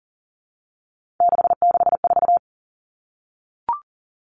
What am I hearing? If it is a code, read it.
Morse code, heard as a keyed tone, about 37 wpm, 700 Hz. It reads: 664
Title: Common Cut Numbers - 3-Digit 40wpm